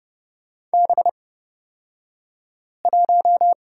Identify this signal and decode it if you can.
Morse code, heard as a keyed tone, about 30 wpm, spaced as for 5 wpm, 700 Hz. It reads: B1
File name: Lesson 37d - Letter x - Two Characters 30wpm